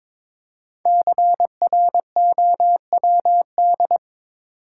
Morse code, heard as a keyed tone, about 22 wpm, 700 Hz. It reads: CROWD
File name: 5 Letter Words - First 3 Letters Given - 22wpm